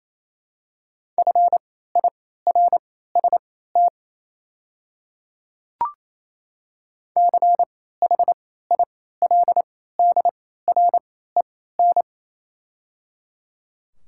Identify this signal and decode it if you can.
Morse code, heard as a keyed tone, about 28 wpm, spaced as for 16 wpm, 700 Hz. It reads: FIRST CHILDREN
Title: 3x Character Spacing - Top 300 Words - 28wpm